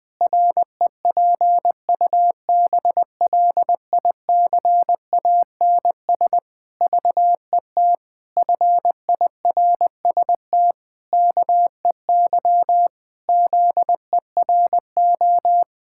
Morse code, heard as a keyed tone, about 20 wpm, 700 Hz. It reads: REPUBLICANS VET FIRST KEY ZERO
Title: Pangram Challenge v2 - 20wpm